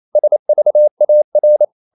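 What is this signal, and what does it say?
Morse code, heard as a keyed tone, about 28 wpm, 600 Hz. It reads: SVAR
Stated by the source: Danish short words sent in Morse at 28 wpm, from CW med Gnister.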